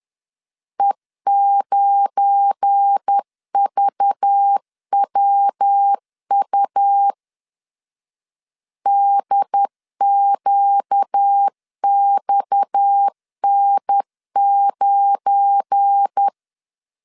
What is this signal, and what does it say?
Morse code, heard as a keyed tone, about 10 wpm, 800 Hz. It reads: E9VWU DQXN9
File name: LEZIONI / 7 GF9Z / 7 GF9 10wpm 2747